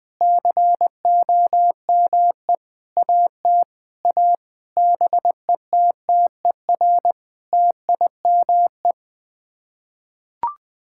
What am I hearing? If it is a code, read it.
Morse code, heard as a keyed tone, about 20 wpm, 700 Hz. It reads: COME AT A BETTER TIME
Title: Sets of 8 Words - 20wpm